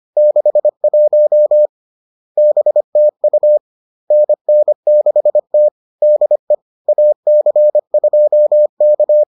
Morse code, heard as a keyed tone, about 25 wpm, 600 Hz. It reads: 61 BTU NN6T DE AC2K